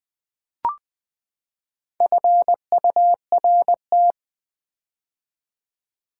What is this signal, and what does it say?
Morse code, heard as a keyed tone, about 20 wpm, 700 Hz. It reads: FURT
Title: Build A Word - 4 Letters At A Time - 11-Letter Words - 20wpm